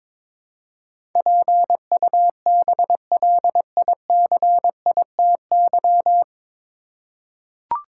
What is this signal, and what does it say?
Morse code, heard as a keyed tone, about 22 wpm, 700 Hz. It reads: PUBLICITY
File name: 9 Letter Words - First 3 Letters Given - 22wpm